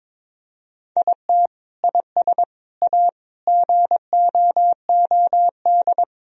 Morse code, heard as a keyed tone, about 22 wpm, 700 Hz. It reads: IT IS A GOOD
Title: Sentences from Top 1000 Words - Review Part 2 - 22wpm